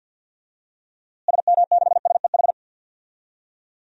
Morse code, heard as a keyed tone, about 53 wpm, 700 Hz. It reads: SM6SEH
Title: DXCC-100 Call Signs - 6 Characters - Repeated 2x - 50wpm